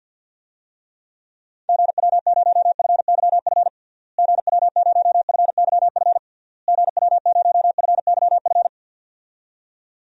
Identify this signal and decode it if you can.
Morse code, heard as a keyed tone, about 51 wpm, 700 Hz. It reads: GW0FYF GW0FYF GW0FYF
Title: DXCC-100 Call Signs - 6 Characters - Repeated 3x - 50wpm